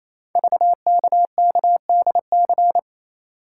Morse code, heard as a keyed tone, about 27 wpm, 700 Hz. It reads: VKKDC
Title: Code Groups - 5 Letters - 28wpm